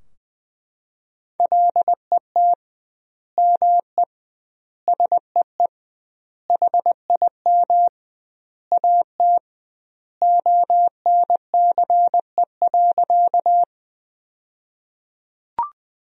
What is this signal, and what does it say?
Morse code, heard as a keyed tone, about 20 wpm, 700 Hz. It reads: LET ME SEE HIM AT ONCE.
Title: 2x Word-Spacing - Sentences from Top 300 Words - 20wpm